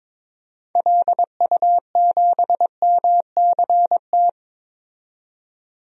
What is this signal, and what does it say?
Morse code, heard as a keyed tone, about 22 wpm, 700 Hz. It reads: LU7MCT
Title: DXCC-100 Call Signs - 6 Characters - 22wpm